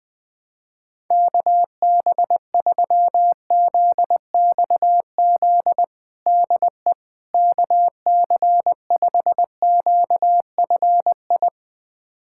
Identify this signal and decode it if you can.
Morse code, heard as a keyed tone, about 20 wpm, 700 Hz. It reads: KB3ZXZ DE KC5QFI
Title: USA Callsign DE Callsign - 6 Characters - 20wpm